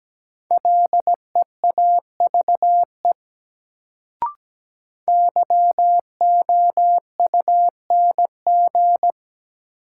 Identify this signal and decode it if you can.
Morse code, heard as a keyed tone, about 17 wpm, 700 Hz. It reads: LEAVE YOUNG
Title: Sentences from Top 400 Words - New Word Drill - 17wpm